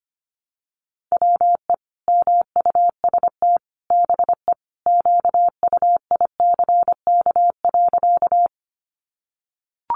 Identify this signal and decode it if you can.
Morse code, heard as a keyed tone, about 25 wpm, 700 Hz. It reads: WE MUST BE QUICK.
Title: Sentences from Top 500 Words 25wpm